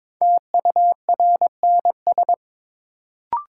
Morse code, heard as a keyed tone, about 22 wpm, 700 Hz. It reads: TURNS
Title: Sets of 5 Words Encore - New Word Drill - 22wpm